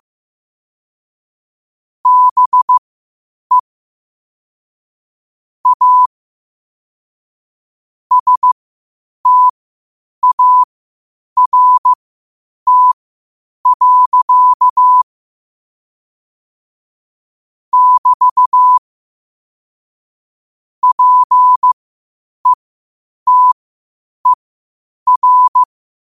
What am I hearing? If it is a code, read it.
Morse code, heard as a keyed tone, about 15 wpm, spaced as for 8 wpm, 1000 Hz. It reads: BE A START. <BT> PETER
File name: qotdfiles / 8wpm